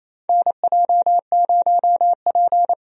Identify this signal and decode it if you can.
Morse code, heard as a keyed tone, about 28 wpm, 700 Hz. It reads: NJ0P
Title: USA Call Signs - 4 Characters - Repeated 3x - 28wpm